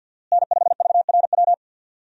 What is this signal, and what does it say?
Morse code, heard as a keyed tone, about 52 wpm, 690 Hz. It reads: N5VRW
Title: USA Call Signs - 5 Characters - Repeated 3x - 50wpm